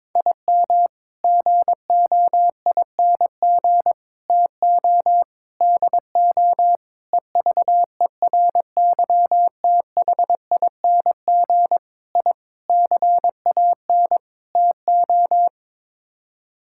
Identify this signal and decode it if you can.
Morse code, heard as a keyed tone, about 22 wpm, 700 Hz. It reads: IM GOING TO DO EVERYTHING I CAN TO
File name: practice / 22wpm / Sets of 8 Words - 22wpm